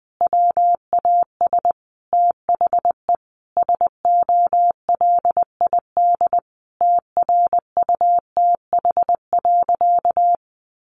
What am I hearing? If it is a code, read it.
Morse code, heard as a keyed tone, about 20 wpm, 700 Hz. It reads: WAS THE SOLID TRUTH.
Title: Sentences from Top 1500 Words 20wpm